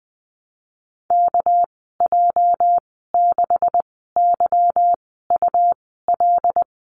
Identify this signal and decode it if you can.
Morse code, heard as a keyed tone, about 20 wpm, spaced as for 15 wpm, 700 Hz. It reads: KJ6YUL
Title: 219 Lesson 35g - Number 0 - Call Signs 20wpm